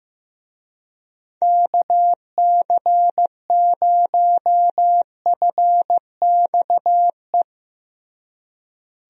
Morse code, heard as a keyed tone, about 15 wpm, 700 Hz. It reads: KC0FXE
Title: USA Call Signs - 6 Characters - Repeated 2x - 15wpm